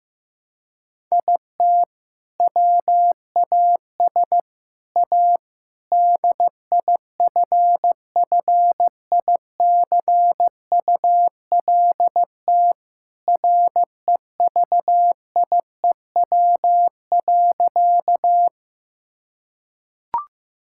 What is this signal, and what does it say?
Morse code, heard as a keyed tone, about 15 wpm, 700 Hz. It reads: IT WAS A DIFFICULT REVIEW.
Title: Sentences from Top 1000 Words - Review Part 6 - 15wpm